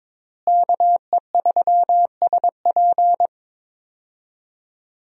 Morse code, heard as a keyed tone, about 22 wpm, 700 Hz. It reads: KE3SP
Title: USA Call Signs - 5 Characters - Repeated 3x - 22wpm